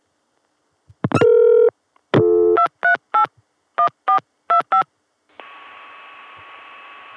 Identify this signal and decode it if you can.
DTMF, heard as telephone touch tones, tones of about 100 ms. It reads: 238 1425